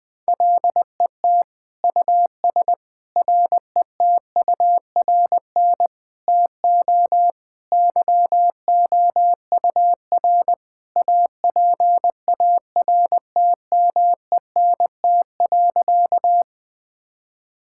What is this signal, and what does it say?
Morse code, heard as a keyed tone, about 20 wpm, 700 Hz. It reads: LET US RETURN TO YOUR APARTMENT.
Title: Sentences from Top 1000 Words - Review Part 6 - 20wpm